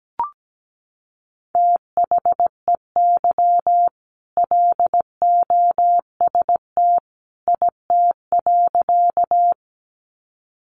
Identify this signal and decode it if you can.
Morse code, heard as a keyed tone, about 17 wpm, 700 Hz. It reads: THEY LOST IT.
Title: Sentences from Top 1400 Words 17wpm